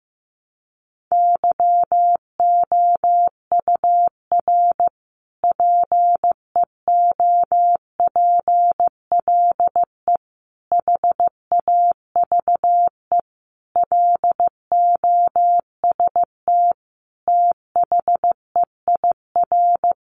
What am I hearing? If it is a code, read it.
Morse code, heard as a keyed tone, about 15 wpm, 700 Hz. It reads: YOUR PEOPLE HAVE LOST THEIR
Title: Sentences from Top 1400 Words 15wpm